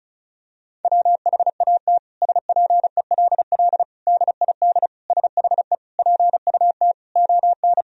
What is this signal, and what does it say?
Morse code, heard as a keyed tone, about 36 wpm, 700 Hz. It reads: WHAT SPELL DID SHE PUT ON
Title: Sentences from Top 1000 Words - Review Part 1 - 35wpm